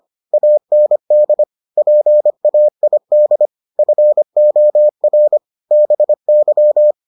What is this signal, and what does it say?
Morse code, heard as a keyed tone, about 25 wpm, 600 Hz. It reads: AND PAID FOR BY